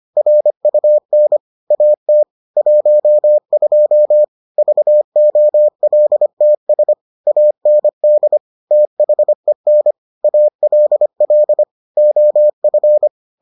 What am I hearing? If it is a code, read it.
Morse code, heard as a keyed tone, about 25 wpm, 600 Hz. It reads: RUN AT 12 VOLTS AND THEN ALL OF